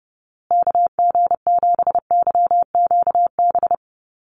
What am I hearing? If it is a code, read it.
Morse code, heard as a keyed tone, about 30 wpm, 700 Hz. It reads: KG7YQB
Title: USA Call Signs Repeated 2x 30wpm